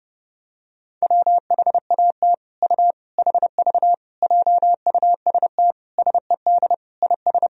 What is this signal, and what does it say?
Morse code, heard as a keyed tone, about 30 wpm, 700 Hz. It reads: WHAT U HV JUST SED IS
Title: Sentences from Top 200 Words and Abbreviations 30wpm